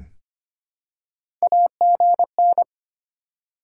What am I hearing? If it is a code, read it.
Morse code, heard as a keyed tone, about 25 wpm, 700 Hz. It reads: AGN
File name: practice / 25wpm / Top 20 QSO Elements - 25wpm